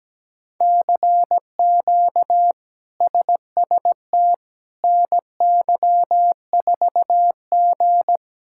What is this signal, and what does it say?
Morse code, heard as a keyed tone, about 17 wpm, 700 Hz. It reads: CQ SST NY4G
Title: K1USN SST QSO Practice - 17wpm